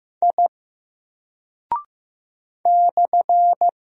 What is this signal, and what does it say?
Morse code, heard as a keyed tone, about 15 wpm, 700 Hz. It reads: I /
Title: Single Character (Letter:Number:Punctuation:Prosign) Alt 15wpm